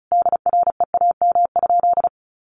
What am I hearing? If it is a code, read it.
Morse code, heard as a keyed tone, about 34 wpm, 700 Hz. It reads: DREAM?
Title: Sentences from Top 1500 Words 35wpm